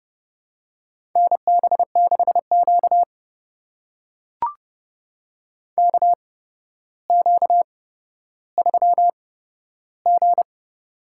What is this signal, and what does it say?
Morse code, heard as a keyed tone, about 30 wpm, spaced as for 13 wpm, 700 Hz. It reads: NB6Q KQ3G
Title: USA Call Signs - 4 Characters - 8x Character Spacing - 30wpm